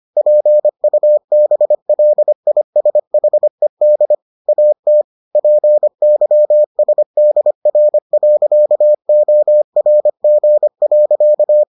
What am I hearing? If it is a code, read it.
Morse code, heard as a keyed tone, about 25 wpm, 600 Hz. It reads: PUBLISHED AT PYSDR.ORG.